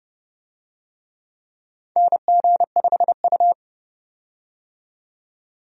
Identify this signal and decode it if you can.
Morse code, heard as a keyed tone, about 30 wpm, 700 Hz. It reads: NG5U
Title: USA Call Signs - 4 Characters - 30wpm